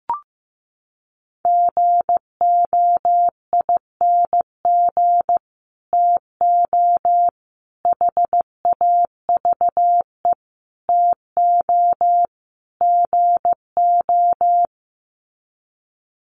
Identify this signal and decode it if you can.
Morse code, heard as a keyed tone, about 15 wpm, 700 Hz. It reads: GOING TO HAVE TO GO
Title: Sets of 5 Words Encore 15wpm